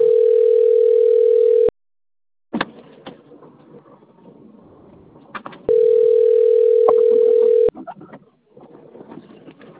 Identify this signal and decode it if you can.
Ringback tone (North America), 440 + 480 Hz.